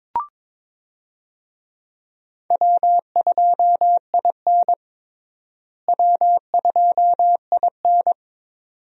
Morse code, heard as a keyed tone, about 22 wpm, 700 Hz. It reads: W2IN W2IN
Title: USA Call Signs - 4 Characters - Repeated 2x - 22wpm